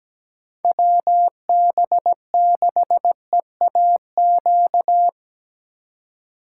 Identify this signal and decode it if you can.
Morse code, heard as a keyed tone, about 20 wpm, 700 Hz. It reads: WB6EAQ
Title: Speed Racing - USA 2x3 Call Signs 17wpm